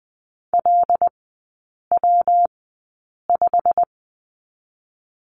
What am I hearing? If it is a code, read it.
Morse code, heard as a keyed tone, about 20 wpm, 700 Hz. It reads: L W 5
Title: 114 Lesson 19e - Letter w - Three Characters 20wpm